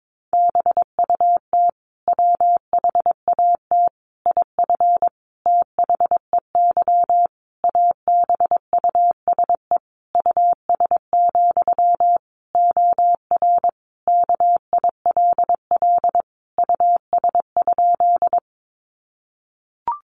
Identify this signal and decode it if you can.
Morse code, heard as a keyed tone, about 22 wpm, 700 Hz. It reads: BUT WHAT IF THEY ABUSE US, OR KILL US?
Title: Sentences from Top 1400 Words 22wpm